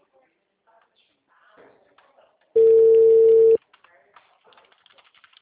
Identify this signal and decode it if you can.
Ringback tone (China, 450 Hz), with speech.